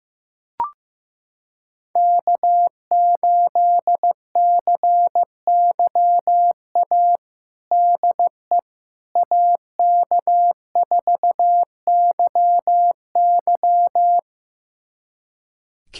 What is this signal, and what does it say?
Morse code, heard as a keyed tone, about 16 wpm, 700 Hz. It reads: K8CYA DE AK4YY
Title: USA Callsign DE Callsign - 5 Characters - 15wpm